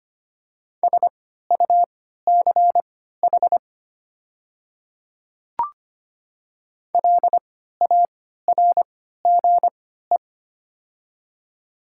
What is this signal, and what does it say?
Morse code, heard as a keyed tone, about 25 wpm, spaced as for 14 wpm, 700 Hz. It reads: SUCH LARGE
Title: 3x Character Spacing - Top 200 Words - 25wpm